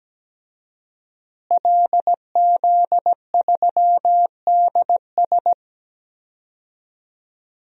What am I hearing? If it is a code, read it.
Morse code, heard as a keyed tone, about 17 wpm, 700 Hz. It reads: LZ3DS
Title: DXCC-100 Call Signs - 5 Characters - Repeated 2x - 17wpm